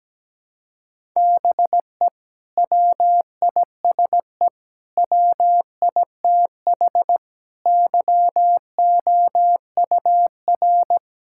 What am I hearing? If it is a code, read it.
Morse code, heard as a keyed tone, about 17 wpm, 700 Hz. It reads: BE WISE WITH YOUR
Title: Sentences from Top 1600 Words 17wpm